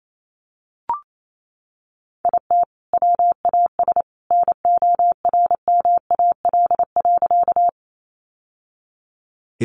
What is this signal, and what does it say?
Morse code, heard as a keyed tone, about 28 wpm, 700 Hz. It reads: IT WAS NORMAL.
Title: Sentences from Top 1200 Words 28wpm